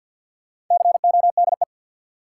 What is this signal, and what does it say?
Morse code, heard as a keyed tone, about 47 wpm, 700 Hz. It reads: CODE
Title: Four Letter Words - 50wpm